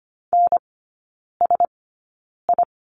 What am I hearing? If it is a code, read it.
Morse code, heard as a keyed tone, about 25 wpm, spaced as for 9 wpm, 690 Hz. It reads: NSI
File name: Lesson 15e - Number 5 - Three Characters 25wpm